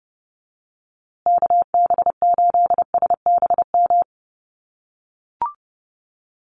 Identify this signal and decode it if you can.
Morse code, heard as a keyed tone, about 30 wpm, 700 Hz. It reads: KB8SBM
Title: USA Call Signs Repeated 2x 30wpm